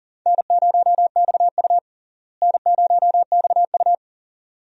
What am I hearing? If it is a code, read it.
Morse code, heard as a keyed tone, about 40 wpm, 700 Hz. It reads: N0XU N0XU
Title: USA Call Signs - 4 Characters - Repeated 2x - 40wpm